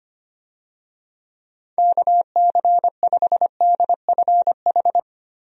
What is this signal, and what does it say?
Morse code, heard as a keyed tone, about 25 wpm, 700 Hz. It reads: KC5DFH
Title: USA Call Signs - 6 Characters - Repeated 3x - 25wpm